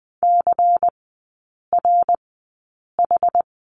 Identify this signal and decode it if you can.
Morse code, heard as a keyed tone, about 20 wpm, spaced as for 8 wpm, 700 Hz. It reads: CRH
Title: 107 Lesson 18e - Letter m - Three Characters 20wpm